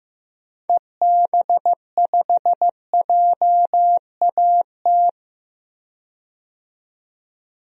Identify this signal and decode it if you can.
Morse code, heard as a keyed tone, about 15 wpm, 700 Hz. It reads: EB5JAT
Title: DXCC-100 Call Signs - 6 Characters - 15wpm